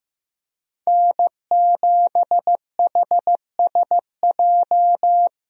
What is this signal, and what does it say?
Morse code, heard as a keyed tone, about 15 wpm, 700 Hz. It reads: N7HSJ
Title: USA Call Signs - 5 Characters - Repeated 3x - 15wpm